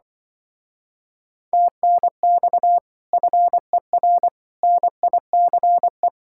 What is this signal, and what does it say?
Morse code, heard as a keyed tone, about 24 wpm, 700 Hz. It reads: TNX FER NICE
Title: Speed-Racing Common QSO Phrases - 20wpm